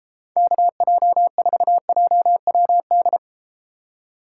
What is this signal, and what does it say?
Morse code, heard as a keyed tone, about 31 wpm, 700 Hz. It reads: KJ4JWD